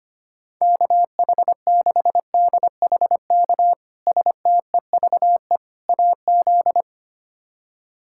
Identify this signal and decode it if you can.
Morse code, heard as a keyed tone, about 25 wpm, 700 Hz. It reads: KH6DHK STEVE AZ
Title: K1USN SST QSO Practice - 25wpm